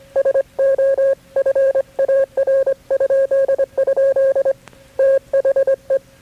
Morse code, heard as a keyed tone, about 24 wpm, 540 Hz. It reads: SO FAR?? THE
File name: KB6MT-tape3-side1